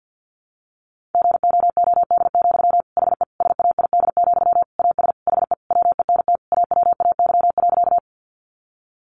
Morse code, heard as a keyed tone, about 48 wpm, 700 Hz. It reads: GOOD, HE SAID, AS HE WENT AWAY.
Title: Sentences from Top 600 Words 50wpm